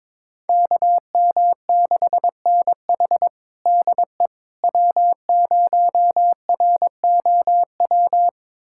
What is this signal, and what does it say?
Morse code, heard as a keyed tone, about 21 wpm, 700 Hz. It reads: KM6NH DE W0ROW
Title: USA Callsign DE Callsign - 5 Characters - 22wpm